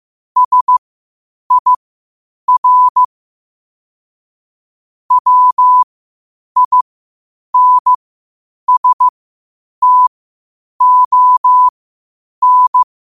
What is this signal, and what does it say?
Morse code, heard as a keyed tone, about 15 wpm, spaced as for 8 wpm, 1000 Hz. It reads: SIR WINSTON
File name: qotdfiles / 8wpm